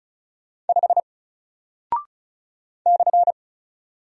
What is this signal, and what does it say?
Morse code, heard as a keyed tone, about 34 wpm, 700 Hz. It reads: 5 /
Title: Single Character (Letter:Number:Punctuation:Prosign) Alt 35wpm